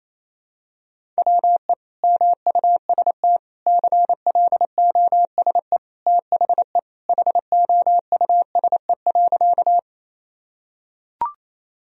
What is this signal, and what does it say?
Morse code, heard as a keyed tone, about 27 wpm, 700 Hz. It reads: WE MUST CLOSE THE HOUSE.
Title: Sentences from Top 1000 Words - Review Part 1 - 28wpm